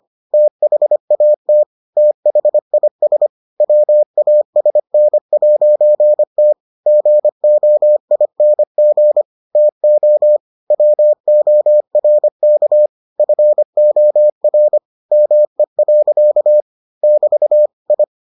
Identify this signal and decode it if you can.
Morse code, heard as a keyed tone, about 25 wpm, 600 Hz. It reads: THAT THIS WASN'T GOING TO WORK FOR ME. <BT> I